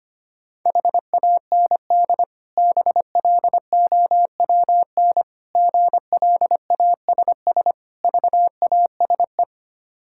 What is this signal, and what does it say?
Morse code, heard as a keyed tone, about 25 wpm, 690 Hz. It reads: HAND BLOWN GLASS VASE
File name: practice / 25wpm / Difficult Words with Hints - 25wpm